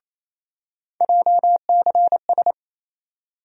Morse code, heard as a keyed tone, about 28 wpm, 700 Hz. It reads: JCS
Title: USA Call Sign Postfixes - 3 Letters 28wpm